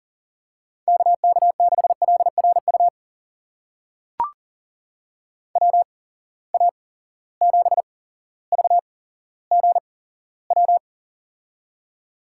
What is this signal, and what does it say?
Morse code, heard as a keyed tone, about 40 wpm, spaced as for 17 wpm, 700 Hz. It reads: KK6LRU WA7VGW
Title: USA Call Signs - 6 Characters - 8x Character Spacing - 40wpm